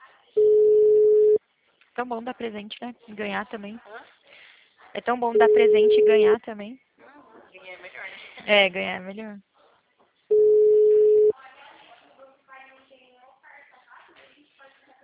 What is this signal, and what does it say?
Ringback tone (Europe, 425 Hz), with speech.